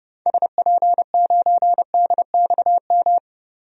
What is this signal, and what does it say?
Morse code, heard as a keyed tone, about 30 wpm, 700 Hz. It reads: SP9DXM
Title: DXCC-100 Call Signs - 6 Characters - 30wpm